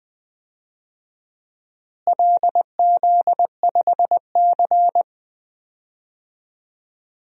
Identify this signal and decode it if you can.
Morse code, heard as a keyed tone, about 20 wpm, 700 Hz. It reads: LZ5C
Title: DXCC-100 Call Signs - 4 Characters - Repeated 3x - 20wpm